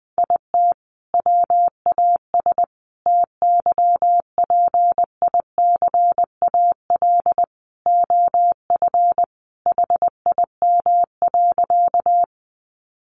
Morse code, heard as a keyed tone, about 20 wpm, 700 Hz. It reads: IT WAS TYPICAL OF HIM.
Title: Sentences from Top 1100 Words 20wpm